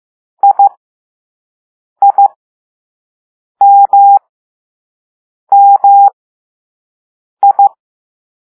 Morse code, heard as a keyed tone, about 15 wpm, spaced as for 5 wpm, 800 Hz. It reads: IIMMI
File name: Morse / Level2A